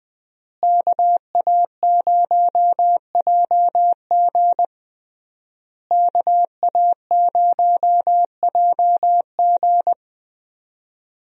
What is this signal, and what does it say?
Morse code, heard as a keyed tone, about 20 wpm, 700 Hz. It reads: KA0JG KA0JG
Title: USA Call Signs - 5 Characters - Repeated 3x - 20wpm